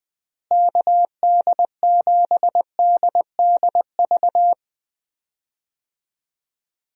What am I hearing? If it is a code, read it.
Morse code, heard as a keyed tone, about 21 wpm, 700 Hz. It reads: KD7DDV
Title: USA Call Signs - 6 Characters - Repeated 2x - 20wpm